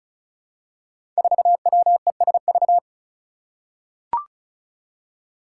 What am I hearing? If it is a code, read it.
Morse code, heard as a keyed tone, about 36 wpm, 700 Hz. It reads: 4WESV
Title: Five Challenging Letter-Number Combinations 35wpm